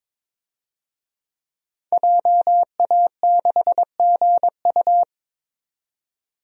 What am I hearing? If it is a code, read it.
Morse code, heard as a keyed tone, about 22 wpm, 700 Hz. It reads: JA6GU
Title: DXCC-100 Call Signs - 5 Characters - 22wpm